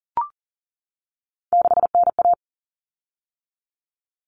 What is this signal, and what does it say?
Morse code, heard as a keyed tone, about 40 wpm, 700 Hz. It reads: 6NA
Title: Common Cut Numbers - 3-Digit 40wpm